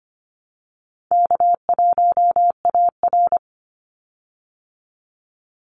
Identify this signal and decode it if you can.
Morse code, heard as a keyed tone, about 25 wpm, 700 Hz. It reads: K1AR
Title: Easy Call Signs - 25wpm